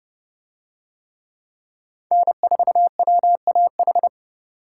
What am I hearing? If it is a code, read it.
Morse code, heard as a keyed tone, about 28 wpm, 700 Hz. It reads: N4WAH